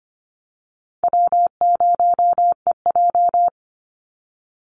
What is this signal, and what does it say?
Morse code, heard as a keyed tone, about 25 wpm, 700 Hz. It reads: W0EJ
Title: Easy Call Signs - 25wpm